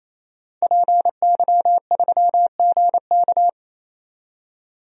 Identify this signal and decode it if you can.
Morse code, heard as a keyed tone, about 28 wpm, 700 Hz. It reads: PY3GK
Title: DXCC-100 Call Signs - 5 Characters - 28wpm